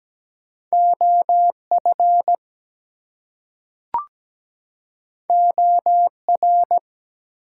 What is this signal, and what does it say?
Morse code, heard as a keyed tone, about 17 wpm, 700 Hz. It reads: OF OR
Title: Two Letter Words - 17wpm